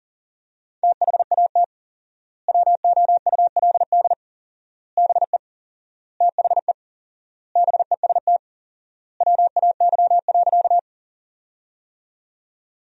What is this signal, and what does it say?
Morse code, heard as a keyed tone, about 40 wpm, 700 Hz. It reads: THAT WOULD BE THE BEST WAY.
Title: 4x Word-Spacing - Sentences from Top 400 Words - 40wpm